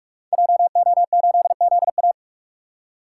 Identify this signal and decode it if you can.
Morse code, heard as a keyed tone, about 44 wpm, 700 Hz. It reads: JO8ZA
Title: Code Groups - 5 Letter:Numbers - 45wpm